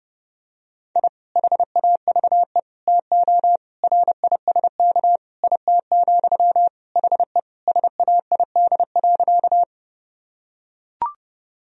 Morse code, heard as a keyed tone, about 28 wpm, 700 Hz. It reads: I HAVE TO RISK IT, HE SAID.
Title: Sentences from Top 1000 Words - Review Part 4 - 30wpm